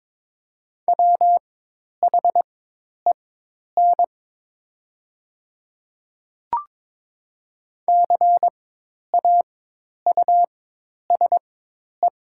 Morse code, heard as a keyed tone, about 22 wpm, spaced as for 10 wpm, 700 Hz. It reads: WHEN CAUSE
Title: 4x Character Spacing - Top 200 Words - 22wpm